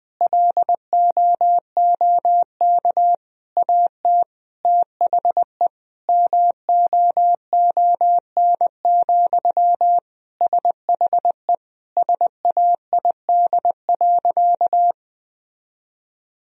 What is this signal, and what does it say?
Morse code, heard as a keyed tone, about 21 wpm, 700 Hz. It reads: LOOK AT THE MOON, SHE SAID.
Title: Sentences from Top 1000 Words - Review Part 3 - 20wpm